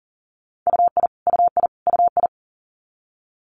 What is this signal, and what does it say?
Morse code, heard as a keyed tone, about 37 wpm, 700 Hz. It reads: UI UI UI
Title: Multi-letter Phonemes Repeated 3x 40wpm